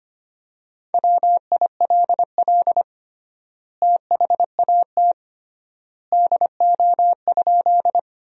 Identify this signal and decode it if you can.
Morse code, heard as a keyed tone, about 25 wpm, 700 Hz. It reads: WILL THAT DO?
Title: 3x Word-Spacing - Sentences from Top 100 Words - 25wpm